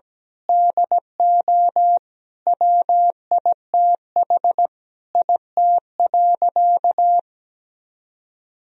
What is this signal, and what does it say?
Morse code, heard as a keyed tone, about 17 wpm, 700 Hz. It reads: DO WITH IT.
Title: Sentences from Top 1000 Words - Review Part 2 - 17wpm